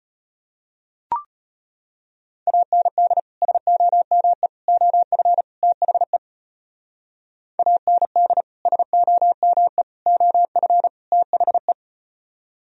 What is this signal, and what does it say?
Morse code, heard as a keyed tone, about 36 wpm, 700 Hz. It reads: AND SOME OF THE AND SOME OF THE